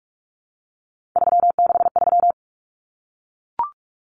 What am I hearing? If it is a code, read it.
Morse code, heard as a keyed tone, about 44 wpm, 700 Hz. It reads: 363
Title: Common Cut Numbers - 3-Digit 45wpm